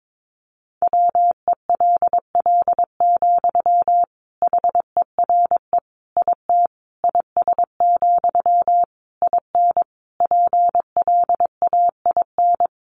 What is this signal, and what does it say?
Morse code, heard as a keyed tone, about 22 wpm, 700 Hz. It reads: WELL, HERE IT IS, IN PLAIN
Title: Sentences from Top 1400 Words 22wpm